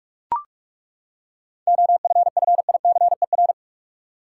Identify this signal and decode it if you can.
Morse code, heard as a keyed tone, about 41 wpm, 700 Hz. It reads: OFFICER
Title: Seven Letter Words - 45wpm